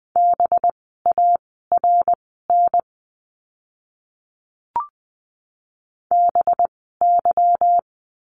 Morse code, heard as a keyed tone, about 20 wpm, spaced as for 14 wpm, 700 Hz. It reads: BARN BY
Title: 185 Lesson 30f - Letter b - Words 20wpm